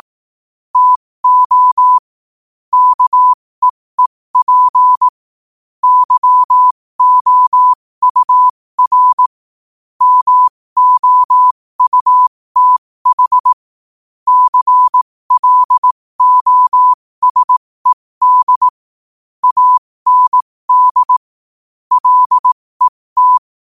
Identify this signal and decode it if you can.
Morse code, heard as a keyed tone, about 18 wpm, spaced as for 15 wpm, 1000 Hz. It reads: TO KEEP YOUR MOUTH CLOSED AND LET
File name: qotdfiles / 15wpm